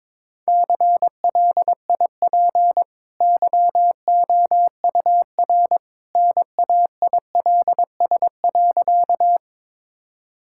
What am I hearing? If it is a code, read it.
Morse code, heard as a keyed tone, about 22 wpm, 690 Hz. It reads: CLIP YOUR NAILS.
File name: practice / 22wpm / Easy Sentences with Hints - 22wpm